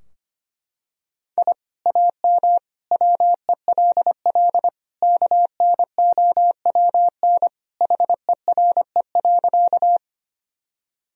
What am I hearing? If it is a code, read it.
Morse code, heard as a keyed tone, about 25 wpm, 700 Hz. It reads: I AM WELL KNOWN HERE.
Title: Sentences from Top 1000 Words - Review Part 3 - 25wpm